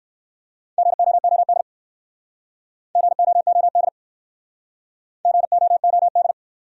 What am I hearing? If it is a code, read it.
Morse code, heard as a keyed tone, about 55 wpm, 700 Hz. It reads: GOOD GOOD GOOD
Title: Speed-Racing Top 200 Words 45wpm